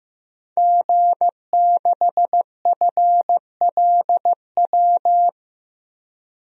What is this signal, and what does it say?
Morse code, heard as a keyed tone, about 15 wpm, 700 Hz. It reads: G6FLW
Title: DXCC-100 Call Signs - 5 Characters - 15wpm